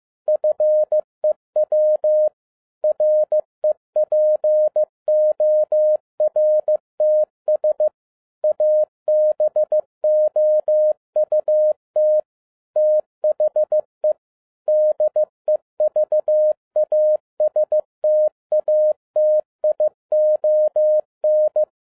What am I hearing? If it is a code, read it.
Morse code, heard as a keyed tone, about 15 wpm, 610 Hz. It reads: FEW REPORTS ABOUT THE DEVASTATION
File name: 2011-10-mp3 / n02-FIELD DAY CONTEST QRV FROM SAKAE VILLAGE 15wpm